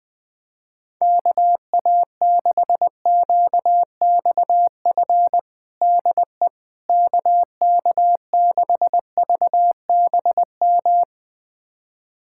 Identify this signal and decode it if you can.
Morse code, heard as a keyed tone, about 20 wpm, 700 Hz. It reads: KA6QXF DE KK6VBM
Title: USA Callsign DE Callsign - 6 Characters - 20wpm